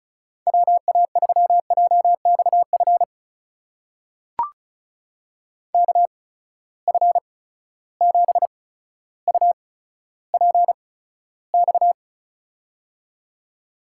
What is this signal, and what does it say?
Morse code, heard as a keyed tone, about 36 wpm, spaced as for 15 wpm, 700 Hz. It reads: WA3JXF KF7UPX
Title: USA Call Signs - 6 Characters - 8x Character Spacing - 35wpm